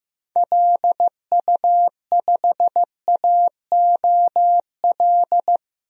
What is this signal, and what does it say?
Morse code, heard as a keyed tone, about 15 wpm, 700 Hz. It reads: LU5AOL
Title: DXCC-100 Call Signs - 6 Characters - 15wpm